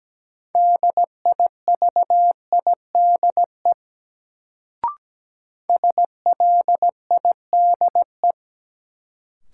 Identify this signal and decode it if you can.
Morse code, heard as a keyed tone, about 17 wpm, 700 Hz. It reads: DIVIDE SLIDE
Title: Sentences from Top 1600 Words - New Word Drill - 17wpm